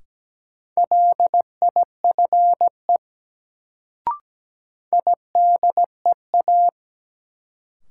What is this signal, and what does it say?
Morse code, heard as a keyed tone, about 17 wpm, 700 Hz. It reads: LIFE IDEA
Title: Sentences from Top 300 Words - New Word Drill - 17wpm